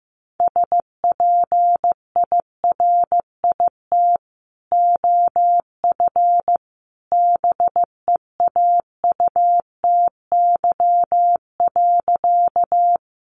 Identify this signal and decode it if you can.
Morse code, heard as a keyed tone, about 14 wpm, 700 Hz. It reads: SPIRIT OF BEAUTY.
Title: Sentences from Top 1400 Words 15wpm